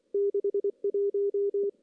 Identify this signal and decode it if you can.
Morse code, heard as a keyed tone, about 25 wpm, 400 Hz. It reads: 61
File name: CW / CW-SS-k5tr- -1700